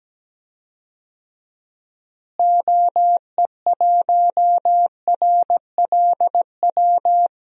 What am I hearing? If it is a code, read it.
Morse code, heard as a keyed tone, about 17 wpm, 700 Hz. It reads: OE1RLW
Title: DXCC-100 Call Signs - 6 Characters - Repeated 3x - 17wpm